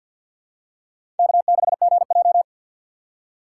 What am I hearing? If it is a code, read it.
Morse code, heard as a keyed tone, about 51 wpm, 700 Hz. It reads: K6GJ
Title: USA Call Signs - 4 Characters - 2x Character Spacing - 50wpm